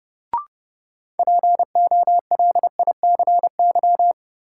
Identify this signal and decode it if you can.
Morse code, heard as a keyed tone, about 30 wpm, 700 Hz. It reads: POLICY
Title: Sentences from Top 1200 Words - New Word Drill Alt - 30wpm